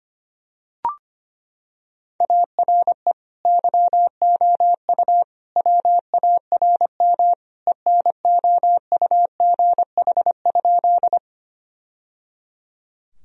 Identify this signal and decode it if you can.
Morse code, heard as a keyed tone, about 25 wpm, 700 Hz. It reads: ARE YOU WARM ENOUGH?
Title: Sentences from Top 1000 Words - Review Part 2 - 25wpm